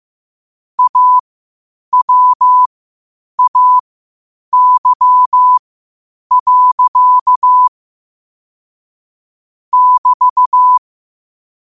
Morse code, heard as a keyed tone, about 15 wpm, spaced as for 8 wpm, 1000 Hz. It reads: AWAY. <BT>
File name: qotdfiles / 8wpm